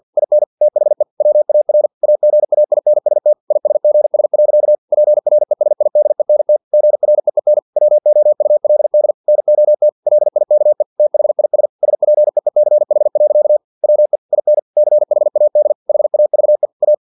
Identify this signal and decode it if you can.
Morse code, heard as a keyed tone, about 50 wpm, 600 Hz. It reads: IN THE WAR AGAINST ISGS. PRESIDENT GREEN WOULD NOT LIKE THIS SPEECH. WE IN CHAD HAVE A
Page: Never 49wpm